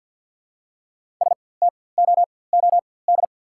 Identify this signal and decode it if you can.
Morse code, heard as a keyed tone, about 48 wpm, spaced as for 23 wpm, 690 Hz. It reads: STOOD
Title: 4x Character Spacing - Top 500 Words - 50wpm